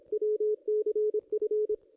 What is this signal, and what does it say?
Morse code, heard as a keyed tone, about 28 wpm, 430 Hz, with radio noise. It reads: WCF
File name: contests / ss-cw-k5tr-op-n5rz -1600